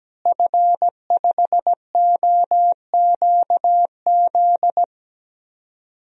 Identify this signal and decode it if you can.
Morse code, heard as a keyed tone, about 17 wpm, 700 Hz. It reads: F5OQZ
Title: DXCC-100 Call Signs - 5 Characters - Repeated 2x - 17wpm